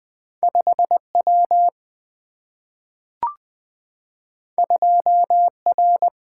Morse code, heard as a keyed tone, about 20 wpm, 700 Hz. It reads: 5 W 2 R
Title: Single-Letter Single-Number 20wpm